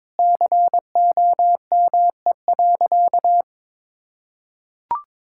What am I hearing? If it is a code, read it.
Morse code, heard as a keyed tone, about 22 wpm, 700 Hz. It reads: COME.
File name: Sentences with 7-Word Phrases - 22wpm